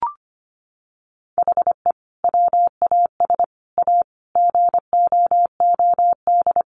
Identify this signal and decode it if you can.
Morse code, heard as a keyed tone, about 25 wpm, 700 Hz. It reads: HE WAS A GOOD
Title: Sentences from Top 1300 Words 25wpm